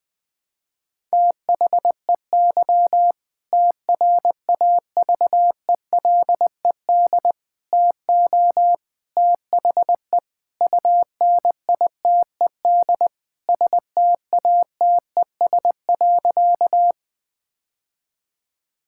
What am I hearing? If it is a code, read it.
Morse code, heard as a keyed tone, about 20 wpm, 700 Hz. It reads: THEY TRAVELED TO THE UNITED STATES.
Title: Sentences with 2-Word Phrases - Spoken Beginning - 20wpm